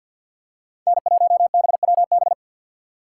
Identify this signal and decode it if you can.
Morse code, heard as a keyed tone, about 52 wpm, 700 Hz. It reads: N1BWB
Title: USA Call Signs - 5 Characters - 4x Character Spacing - 50wpm